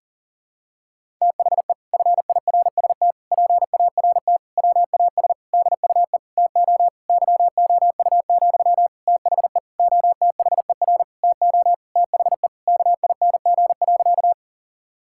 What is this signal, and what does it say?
Morse code, heard as a keyed tone, about 40 wpm, 700 Hz. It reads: THE FIRST PART WAS DUE TO YOU, THE OTHER TO THE KING.
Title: Sentences from Top 1000 Words - Review Part 6 - 40wpm